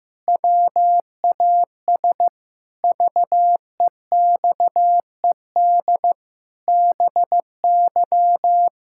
Morse code, heard as a keyed tone, about 15 wpm, 700 Hz. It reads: WAS VEXED BY
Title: Pangram Challenge v2 - 15wpm